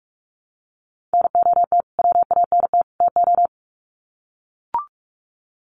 Morse code, heard as a keyed tone, about 44 wpm, 700 Hz. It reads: NOT WANT TO
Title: Sets of 3 Words 45wpm